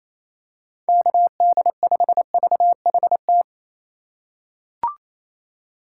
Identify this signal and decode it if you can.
Morse code, heard as a keyed tone, about 28 wpm, 700 Hz. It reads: KD5VHT
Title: USA Call Signs - 6 Characters - 6x Character Spacing - 28wpm